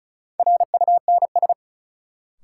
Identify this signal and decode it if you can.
Morse code, heard as a keyed tone, about 36 wpm, 700 Hz. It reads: RUNS
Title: Sentences from Top 600 Words - New Word Drill - 35wpm